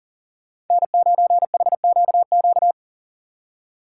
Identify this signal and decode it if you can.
Morse code, heard as a keyed tone, about 49 wpm, 700 Hz. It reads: N9HQQ
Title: Speed Racing - USA 1x3 Call Signs 40wpm